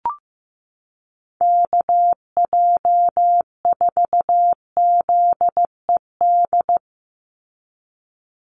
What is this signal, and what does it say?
Morse code, heard as a keyed tone, about 15 wpm, 700 Hz. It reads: KJ4ZED
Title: USA Call Signs 15wpm